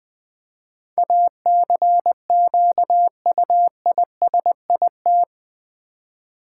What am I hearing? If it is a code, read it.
Morse code, heard as a keyed tone, about 20 wpm, 700 Hz. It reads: ACQUISIT
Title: Build A Word - 4 Letters At A Time - 11-Letter Words - 20wpm